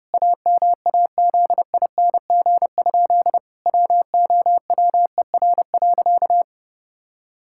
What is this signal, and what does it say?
Morse code, heard as a keyed tone, about 30 wpm, 700 Hz. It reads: AMAZING? WOWER.
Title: Pangram Challenge v2 - 30wpm